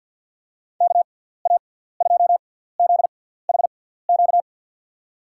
Morse code, heard as a keyed tone, about 50 wpm, spaced as for 17 wpm, 700 Hz. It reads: KA2ZHQ
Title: USA Call Signs - 6 Characters - 6x Character Spacing - 50wpm